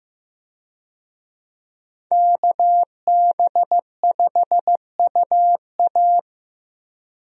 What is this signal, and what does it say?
Morse code, heard as a keyed tone, about 15 wpm, 700 Hz. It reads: KB5UA
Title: USA Call Signs - 5 Characters - Repeated 3x - 15wpm